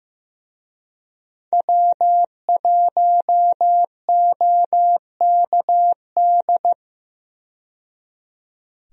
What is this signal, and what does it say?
Morse code, heard as a keyed tone, about 15 wpm, 700 Hz. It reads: W1OKD
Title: USA Call Signs - 5 Characters - 15wpm